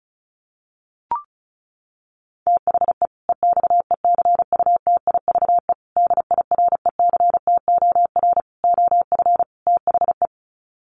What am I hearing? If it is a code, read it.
Morse code, heard as a keyed tone, about 34 wpm, 700 Hz. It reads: THE EXECUTIVE DIRECTOR OF THE
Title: Sets of 5 Words Encore 35wpm